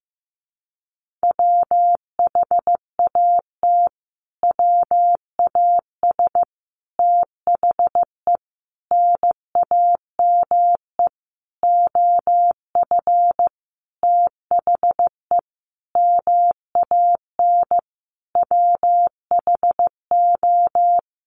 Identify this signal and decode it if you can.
Morse code, heard as a keyed tone, about 15 wpm, 700 Hz. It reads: WHAT WAS THE NAME OF THE MAN WHO
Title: Sentences from Top 300 Words 15wpm